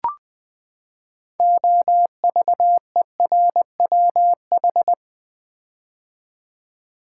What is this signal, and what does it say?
Morse code, heard as a keyed tone, about 20 wpm, 700 Hz. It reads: OVERWH
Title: Build A Word - 6 Letters At A Time - 12-Letter Words - 20wpm